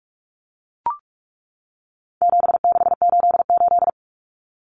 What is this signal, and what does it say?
Morse code, heard as a keyed tone, about 44 wpm, 700 Hz. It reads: 7688
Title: Common Cut Numbers - 4-Digit 45wpm